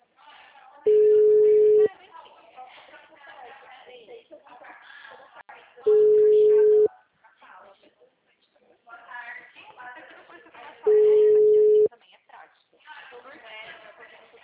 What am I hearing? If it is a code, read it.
Ringback tone (Europe, 425 Hz), with speech.
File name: out- -2011- - - .37309